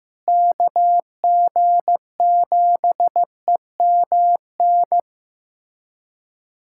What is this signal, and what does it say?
Morse code, heard as a keyed tone, about 14 wpm, 700 Hz. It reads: KG7EMN
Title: USA Callsign DE Callsign - 6 Characters - 15wpm